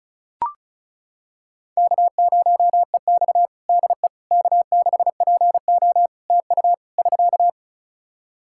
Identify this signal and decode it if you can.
Morse code, heard as a keyed tone, about 34 wpm, 700 Hz. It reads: K0EX DE K6PO TU <SK>
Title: Practice QSO - Exchange 4 - Ending - 35wpm